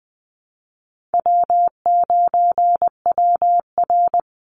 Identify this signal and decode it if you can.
Morse code, heard as a keyed tone, about 20 wpm, 700 Hz. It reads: W9WR
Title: Easy Call Signs - 20wpm